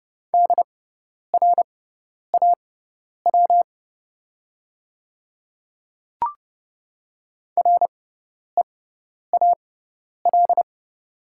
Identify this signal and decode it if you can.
Morse code, heard as a keyed tone, about 30 wpm, spaced as for 10 wpm, 700 Hz. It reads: DRAW REAL
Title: 6x Character Spacing - Top 300 Words - 30wpm